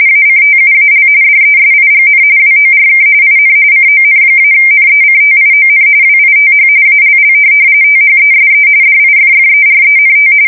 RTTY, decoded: ACTUAL DESIGN WORK IN CON-NECTION WITH ALL THESE DIFFERENT

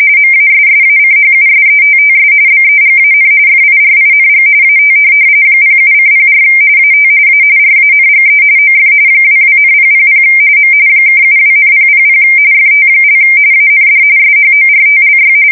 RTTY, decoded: MASSACHUSETTS BEGAN FIRING AT 07:04 AT A RANGE OF 22,000 M (72,000 FT) AND SHORTLY

